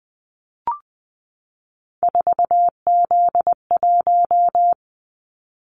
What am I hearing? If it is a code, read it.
Morse code, heard as a keyed tone, about 20 wpm, 700 Hz. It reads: 4Z1
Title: Callsign Prefixes for Top 50 DXCC - 20wpm